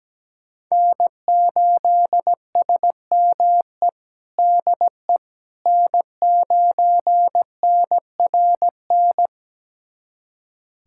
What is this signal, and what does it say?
Morse code, heard as a keyed tone, about 17 wpm, 700 Hz. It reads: N8SME DE N9NRN
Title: USA Callsign DE Callsign - 5 Characters - 17wpm